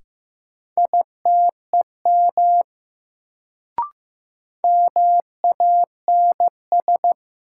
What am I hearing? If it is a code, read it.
Morse code, heard as a keyed tone, about 15 wpm, 700 Hz. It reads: ITEM MANS
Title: Sentences from Top 1000 Words - Review Part 6 - New Word Drill - 15wpm